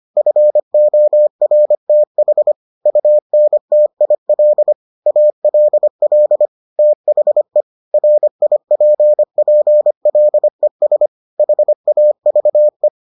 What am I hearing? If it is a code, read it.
Morse code, heard as a keyed tone, about 25 wpm, 600 Hz. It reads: FORTH UNTIL ALL THE RIPPLES HAVE